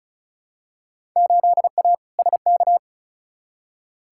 Morse code, heard as a keyed tone, about 36 wpm, 700 Hz. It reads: 8A SK
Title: Field Day Exchanges Encore - 35wpm